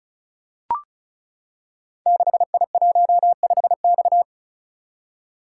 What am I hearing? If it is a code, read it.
Morse code, heard as a keyed tone, about 36 wpm, 690 Hz. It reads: 6I15X
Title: Code Groups - 5 Letter:Numbers - 35wpm